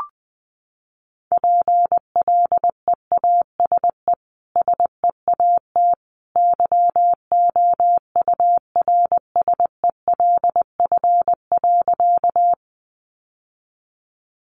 Morse code, heard as a keyed tone, about 20 wpm, 700 Hz. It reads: PLEASE SEAT YOURSELF.
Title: Sentences from Top 1500 Words 20wpm